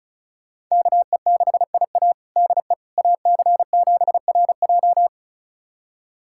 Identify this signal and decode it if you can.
Morse code, heard as a keyed tone, about 36 wpm, 700 Hz. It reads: KE6IA DE AC7RJ
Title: USA Callsign DE Callsign - 5 Characters - 35wpm